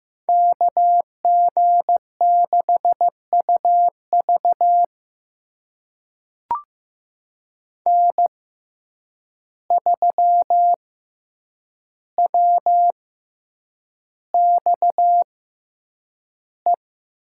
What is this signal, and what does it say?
Morse code, heard as a keyed tone, about 15 wpm, spaced as for 8 wpm, 700 Hz. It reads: KG6UV N3WXE
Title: USA Call Signs - 5 Characters - 6x Character Spacing - 15wpm